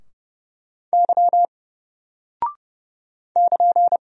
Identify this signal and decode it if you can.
Morse code, heard as a keyed tone, about 29 wpm, 700 Hz. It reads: Y <KN>
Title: Single Character (Letter:Number:Punctuation:Prosign) 30wpm